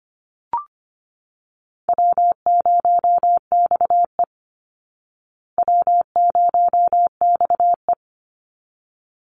Morse code, heard as a keyed tone, about 25 wpm, 700 Hz. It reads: W0XE W0XE
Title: Easy Call Signs Repeated 2x - 25wpm